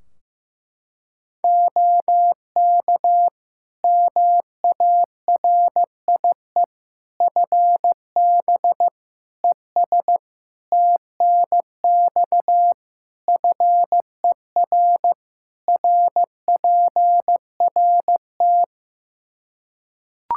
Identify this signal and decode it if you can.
Morse code, heard as a keyed tone, about 15 wpm, 700 Hz. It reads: OK MARIE FB ES TNX FER RPRT
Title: Practice QSO - Exchange 2 - Info - 15wpm